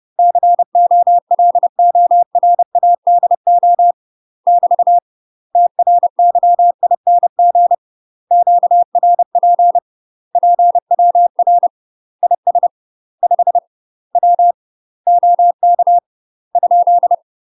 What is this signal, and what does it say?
Morse code, heard as a keyed tone, about 30 wpm, 700 Hz. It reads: COLORADO <BT> TRYING QRP PWR IS 5 W OK ?